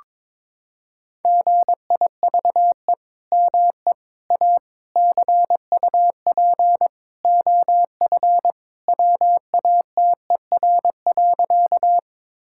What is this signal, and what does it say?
Morse code, heard as a keyed tone, about 22 wpm, 700 Hz. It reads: GIVE ME A CUP OF WATER.
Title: Sentences from Top 1600 Words 22wpm